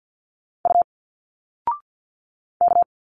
Morse code, heard as a keyed tone, about 50 wpm, 700 Hz. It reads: U K
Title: Single Letters 50wpm